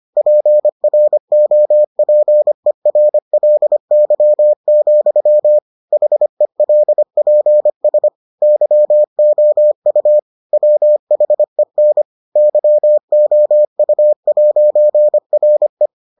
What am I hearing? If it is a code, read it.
Morse code, heard as a keyed tone, about 25 wpm, 600 Hz. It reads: PROPERLY, HELPS YOU WHEN YOU'RE